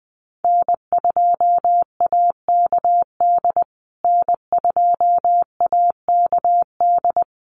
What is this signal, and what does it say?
Morse code, heard as a keyed tone, about 20 wpm, 700 Hz. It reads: N2AKD N2AKD
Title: USA Call Signs Repeated 3x 20wpm